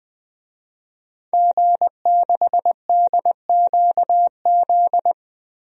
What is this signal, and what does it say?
Morse code, heard as a keyed tone, about 21 wpm, 700 Hz. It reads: G6DQZ
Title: DXCC-100 Call Signs - 5 Characters - Repeated 2x - 20wpm